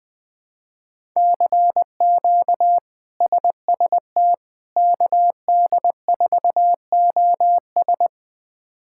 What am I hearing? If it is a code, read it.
Morse code, heard as a keyed tone, about 20 wpm, 700 Hz. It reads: CQ SST KD4OS
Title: K1USN SST QSO Practice - 20wpm